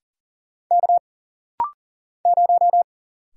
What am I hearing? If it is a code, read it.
Morse code, heard as a keyed tone, about 40 wpm, 700 Hz. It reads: K 0
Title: Single Letter-Number - Rapid-Fire 40wpm